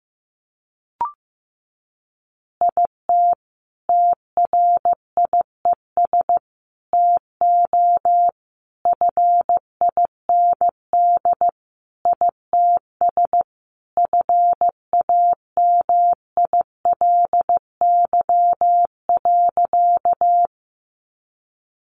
Morse code, heard as a keyed tone, about 15 wpm, 700 Hz. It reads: IT TRIES TO FIND ITS FAMILY.
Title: Additional Easy Phrases 15wpm